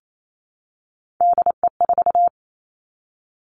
Morse code, heard as a keyed tone, about 28 wpm, 700 Hz. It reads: DE4
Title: Three Challenging Letter-Number Combinations 28wpm